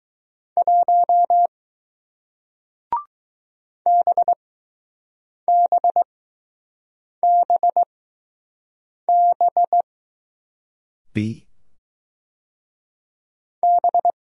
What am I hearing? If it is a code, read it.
Morse code, heard as a keyed tone, about 20 wpm, spaced as for 6 wpm, 700 Hz. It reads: 1 BBBB B